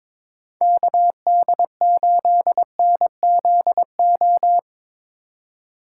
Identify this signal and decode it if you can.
Morse code, heard as a keyed tone, about 22 wpm, 700 Hz. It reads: KD8NZO
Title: USA Call Signs - 6 Characters - 3x Character Spacing - 22wpm